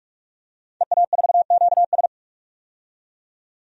Morse code, heard as a keyed tone, about 46 wpm, 700 Hz. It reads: EA4QS
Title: DXCC-100 Call Signs - 5 Characters - 45wpm